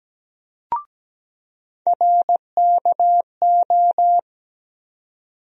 Morse code, heard as a keyed tone, about 17 wpm, 700 Hz. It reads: RKO
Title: USA Call Sign Postfixes - 3 Letters 17wpm